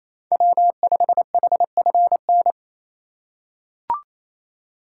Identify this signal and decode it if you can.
Morse code, heard as a keyed tone, about 28 wpm, 700 Hz. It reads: W5HFN
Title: USA Call Signs - 5 Characters - 2x Character Spacing - 28wpm